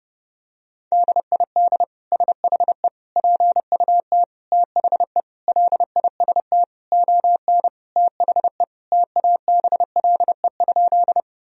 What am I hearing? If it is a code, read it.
Morse code, heard as a keyed tone, about 30 wpm, 700 Hz. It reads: DID SHE PUT THE LIST ON THE TABLE?
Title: Sentences from Top 1000 Words - Review Part 2 - 30wpm